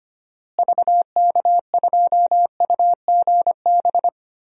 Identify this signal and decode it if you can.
Morse code, heard as a keyed tone, about 25 wpm, 700 Hz. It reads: VK2UGB
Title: DXCC-100 Call Signs - 6 Characters - Repeated 3x - 25wpm